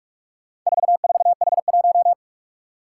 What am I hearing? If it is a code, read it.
Morse code, heard as a keyed tone, about 45 wpm, 700 Hz. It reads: 44H1
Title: Four Challenging Letter-Number Combinations 45wpm